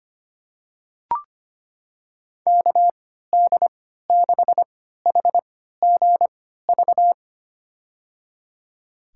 Morse code, heard as a keyed tone, about 25 wpm, spaced as for 14 wpm, 700 Hz. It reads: KD6HGV
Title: USA Call Signs - 6 Characters - 3x Character Spacing - 25wpm